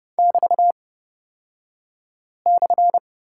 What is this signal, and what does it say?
Morse code, heard as a keyed tone, about 30 wpm, spaced as for 5 wpm, 700 Hz. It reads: <BT>/
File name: Lesson 36d - BT - Two Characters 30wpm